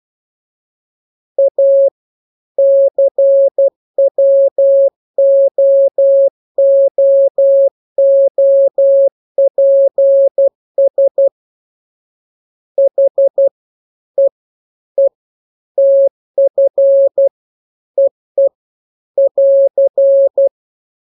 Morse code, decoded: A CWOOOPS HEE TF EE <AR>